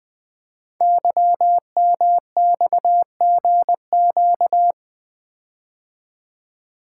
Morse code, heard as a keyed tone, about 20 wpm, 700 Hz. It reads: YMXGQ
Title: Code Groups - 5 Letters - 20wpm